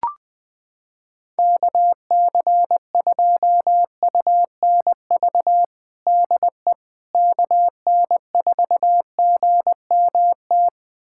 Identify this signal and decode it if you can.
Morse code, heard as a keyed tone, about 20 wpm, 700 Hz. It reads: KC2UNV DE KN4GMT
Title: USA Callsign DE Callsign - 6 Characters - 20wpm